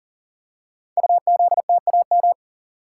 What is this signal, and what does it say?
Morse code, heard as a keyed tone, about 40 wpm, 700 Hz. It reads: UZTAM
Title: Code Groups - 5 Letters - 40wpm